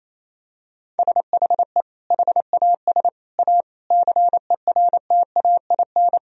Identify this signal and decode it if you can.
Morse code, heard as a keyed tone, about 28 wpm, 700 Hz. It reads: SHE HAS A CERTAIN
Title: Sentences from Top 1000 Words - Review Part 4 - 28wpm